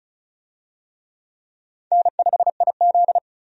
Morse code, heard as a keyed tone, about 34 wpm, 700 Hz. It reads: N5IZ
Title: USA Call Signs - 4 Characters - Repeated 2x - 35wpm